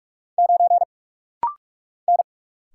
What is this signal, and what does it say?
Morse code, heard as a keyed tone, about 46 wpm, 700 Hz. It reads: 9 N
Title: Single Letter-Number Alt - Rapid-Fire 45wpm